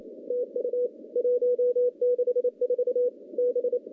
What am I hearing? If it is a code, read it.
Morse code, heard as a keyed tone, about 27 wpm, 490 Hz. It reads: TU 164 B
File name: contests / CW-SS-k5tr- -0000